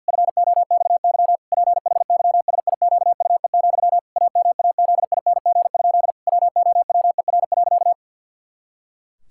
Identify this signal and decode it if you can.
Morse code, heard as a keyed tone, about 50 wpm, 700 Hz. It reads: FOXY PHYSIQUE, AMAZING? WOWER.
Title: Pangram Challenge v2 - 50wpm